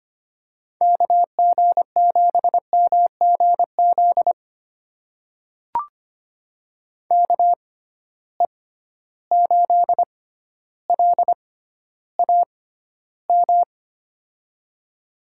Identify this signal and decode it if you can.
Morse code, heard as a keyed tone, about 25 wpm, spaced as for 13 wpm, 700 Hz. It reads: KG7MGZ KE8LAM
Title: USA Call Signs - 6 Characters - 6x Character Spacing - 25wpm